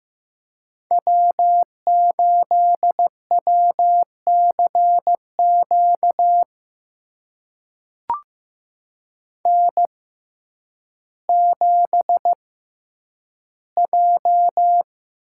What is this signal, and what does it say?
Morse code, heard as a keyed tone, about 15 wpm, spaced as for 9 wpm, 700 Hz. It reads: W8WCQ N7J
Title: USA Call Signs - 5 Characters - 6x Character Spacing - 15wpm